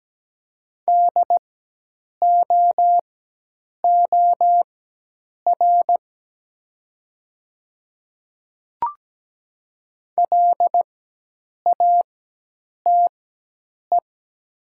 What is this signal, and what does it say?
Morse code, heard as a keyed tone, about 17 wpm, spaced as for 8 wpm, 700 Hz. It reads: DOOR LATE
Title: 4x Character Spacing - Top 300 Words - 17wpm